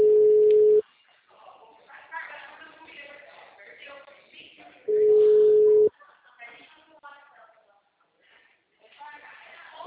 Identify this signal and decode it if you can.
Ringback tone (Europe, 425 Hz), with speech.